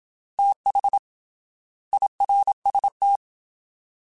Morse code, heard as a keyed tone, about 27 wpm, 800 Hz. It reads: TH IRST
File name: TH / TH mp3s / Fast TH @25